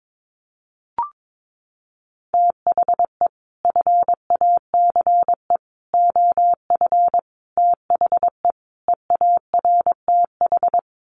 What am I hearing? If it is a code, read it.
Morse code, heard as a keyed tone, about 22 wpm, 700 Hz. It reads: THE FACE OF THE EARTH
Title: Sets of 5 Words Encore 22wpm